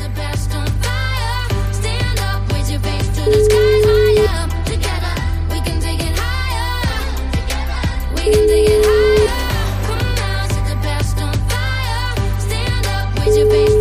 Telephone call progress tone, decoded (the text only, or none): ringback tone